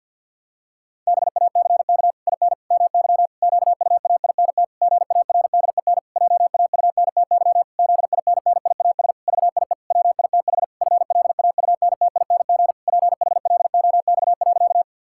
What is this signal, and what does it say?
Morse code, heard as a keyed tone, about 52 wpm, 700 Hz. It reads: BACK IN MY QUAINT GARDEN JAUNTY ZINNIAS VIE WITH FLAUNTING PHLOX.
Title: Pangram Challenge v2 - 50wpm